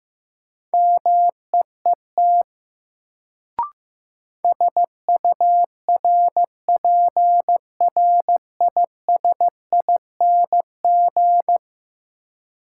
Morse code, heard as a keyed tone, about 15 wpm, 700 Hz. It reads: MEET SURPRISING
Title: Sets of 5 Words - New Word Drill - 15wpm